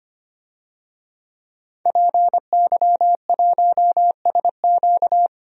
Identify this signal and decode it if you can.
Morse code, heard as a keyed tone, about 25 wpm, 700 Hz. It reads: PY1SQ
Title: DXCC-100 Call Signs - 5 Characters - 25wpm